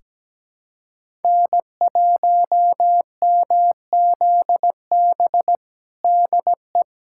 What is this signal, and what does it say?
Morse code, heard as a keyed tone, about 17 wpm, 700 Hz. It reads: N1MZB DE
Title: USA Callsign DE Callsign - 5 Characters - 17wpm